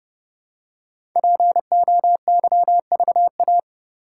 Morse code, heard as a keyed tone, about 28 wpm, 700 Hz. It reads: POYVA
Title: Code Groups - 5 Letters - 30wpm